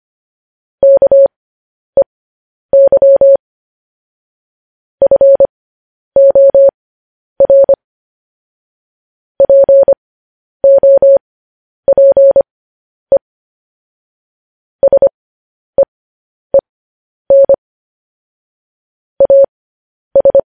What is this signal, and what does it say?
Morse code, heard as a keyed tone, about 25 wpm, spaced as for 10 wpm, 550 Hz. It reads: KEY FOR POPE SEEN AS
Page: news-headlines-in-morse-code-at-10-wpm